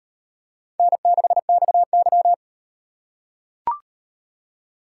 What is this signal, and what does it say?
Morse code, heard as a keyed tone, about 35 wpm, 700 Hz. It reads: N6XY